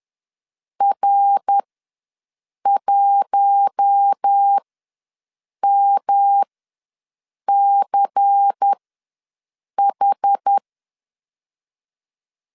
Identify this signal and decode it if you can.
Morse code, heard as a keyed tone, about 10 wpm, spaced as for 6 wpm, 800 Hz. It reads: R1MCH
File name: LEZIONI / 5 LBVC / 5 LBVC ws 10wpm 3733